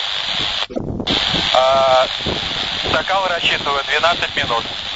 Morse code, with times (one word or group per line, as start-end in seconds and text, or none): none